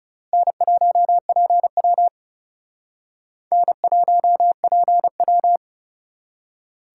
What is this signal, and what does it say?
Morse code, heard as a keyed tone, about 39 wpm, 700 Hz. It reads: N1PW N1PW